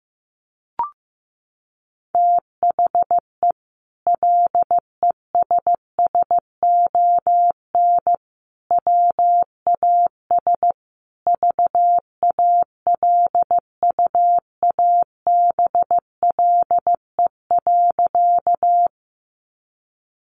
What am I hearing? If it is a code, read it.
Morse code, heard as a keyed tone, about 15 wpm, 700 Hz. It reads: THE LESSON WAS VALUABLE.
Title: Sentences from Top 1200 Words 15wpm